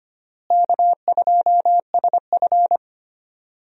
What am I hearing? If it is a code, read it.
Morse code, heard as a keyed tone, about 25 wpm, 700 Hz. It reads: K2SF
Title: USA Call Signs - 4 Characters - Repeated 3x - 25wpm